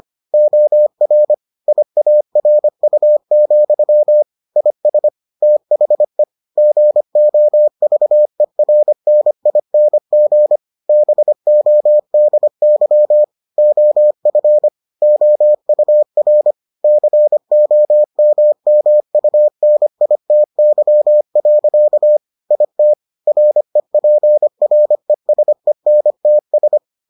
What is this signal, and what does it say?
Morse code, heard as a keyed tone, about 25 wpm, 600 Hz. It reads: OR IARU, IS THE GOVERNING BODY OF OUR COMMUNITY. IT REPRESENTS